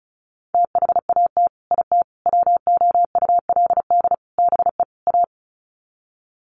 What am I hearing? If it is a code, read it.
Morse code, heard as a keyed tone, about 34 wpm, 700 Hz. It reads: THAT IT WOULD BE A
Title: Sets of 5 Words Encore2 35wpm